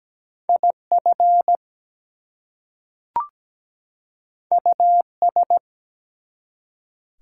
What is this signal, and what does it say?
Morse code, heard as a keyed tone, about 17 wpm, 700 Hz. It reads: IF US
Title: Two Letter Words - 17wpm